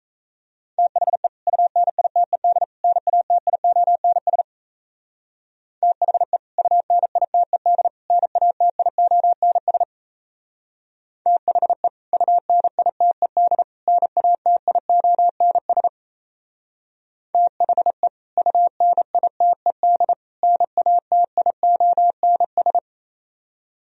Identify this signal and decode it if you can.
Morse code, heard as a keyed tone, about 34 wpm, 700 Hz. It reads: THE UNITED NATIONS THE UNITED NATIONS THE UNITED NATIONS THE UNITED NATIONS
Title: Speed-Racing - Sets of 3 Words 28wpm